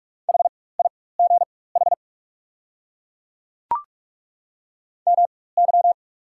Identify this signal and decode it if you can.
Morse code, heard as a keyed tone, about 43 wpm, spaced as for 21 wpm, 690 Hz. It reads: HIGH MY
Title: 4x Character Spacing - Top 200 Words - 45wpm